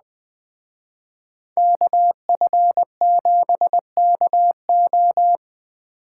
Morse code, heard as a keyed tone, about 20 wpm, 700 Hz. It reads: KF7KO
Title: Speed Racing - USA 2x2 Call Signs 17wpm